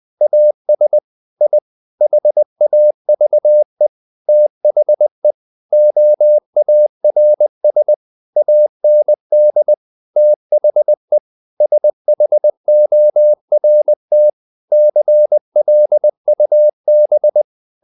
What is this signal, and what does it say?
Morse code, heard as a keyed tone, about 20 wpm, 600 Hz. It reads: AS I HAVE THE OARS AND THE SHORT CLUB
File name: the old man and the sea part 2 20wpm